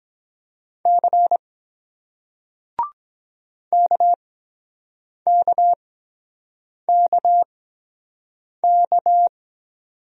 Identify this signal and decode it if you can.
Morse code, heard as a keyed tone, about 23 wpm, spaced as for 7 wpm, 700 Hz. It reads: C KKKK